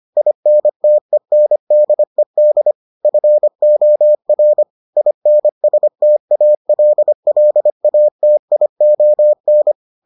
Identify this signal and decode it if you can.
Morse code, heard as a keyed tone, about 25 wpm, 600 Hz. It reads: INTENDED FOR INSTALLATION